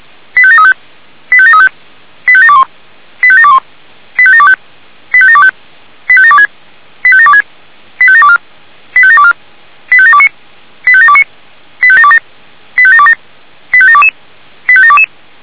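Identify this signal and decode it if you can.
Five-tone selective call (ZVEI), heard as quick runs of five tones, tones of about 70 ms. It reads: 85735 85735 85721 85721 85725 85725 85726 85726 85724 85724 85729 85729 85727 85727 85722 85722
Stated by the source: ZVEI five-tone call received by SDR.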